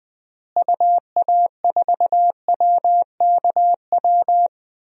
Morse code, heard as a keyed tone, about 20 wpm, 700 Hz. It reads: UA4WKW
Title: DXCC-100 Call Signs - 6 Characters - Repeated 3x - 20wpm